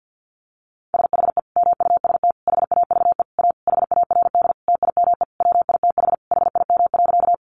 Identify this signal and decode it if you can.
Morse code, heard as a keyed tone, about 50 wpm, 700 Hz. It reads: SHE MUST HAVE A HARD TIME WITH HIM.
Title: Sentences from Top 300 Words 50wpm